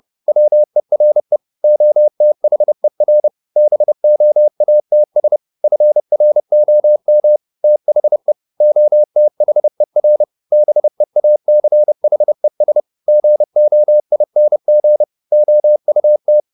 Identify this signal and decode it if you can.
Morse code, heard as a keyed tone, about 30 wpm, 600 Hz. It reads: WERE OTHER BOATS FROM THE OTHER BEACHES GOING OUT
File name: the old man and the sea / the old man and the sea 1 30wpm